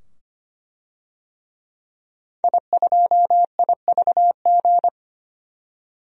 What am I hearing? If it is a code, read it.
Morse code, heard as a keyed tone, about 25 wpm, 700 Hz. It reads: I2IVG
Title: DXCC-100 Call Signs - 5 Characters - 25wpm